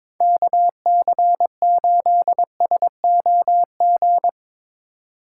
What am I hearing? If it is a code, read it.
Morse code, heard as a keyed tone, about 22 wpm, 700 Hz. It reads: KC8SOG
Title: USA Call Signs - 2x Character Spacing - 22wpm